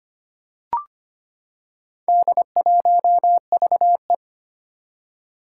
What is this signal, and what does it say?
Morse code, heard as a keyed tone, about 25 wpm, 700 Hz. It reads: D1VE
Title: Four Challenging Letter-Number Combinations 25wpm